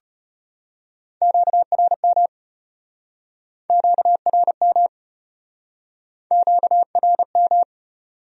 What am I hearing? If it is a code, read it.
Morse code, heard as a keyed tone, about 36 wpm, 700 Hz. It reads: QRM QRM QRM